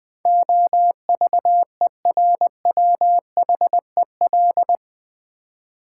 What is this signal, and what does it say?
Morse code, heard as a keyed tone, about 20 wpm, 700 Hz. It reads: OVERWHEL
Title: Build A Word - 4 Letters At A Time - 12-Letter Words - 20wpm